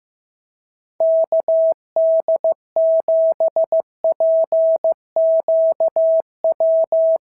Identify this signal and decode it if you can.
Morse code, heard as a keyed tone, about 15 wpm, 690 Hz. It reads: KD7PQW
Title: SOTA QSO Practice 15wpm